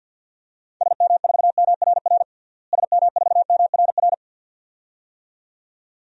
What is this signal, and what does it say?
Morse code, heard as a keyed tone, about 48 wpm, 700 Hz. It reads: SM4MRR SM4MRR
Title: DXCC-100 Call Signs - 6 Characters - Repeated 2x - 50wpm